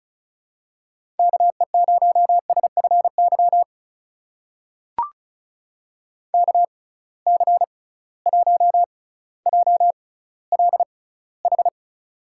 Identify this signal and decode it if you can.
Morse code, heard as a keyed tone, about 36 wpm, spaced as for 18 wpm, 700 Hz. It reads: KE0SFY KC1JLH
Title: USA Call Signs - 6 Characters - 6x Character Spacing - 35wpm